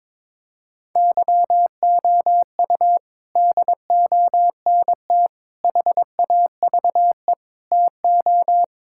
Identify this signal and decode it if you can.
Morse code, heard as a keyed tone, about 22 wpm, 700 Hz. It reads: YOU DONT HAVE TO
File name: practice / 22wpm / Sentences with 8-Word Phrases - 22wpm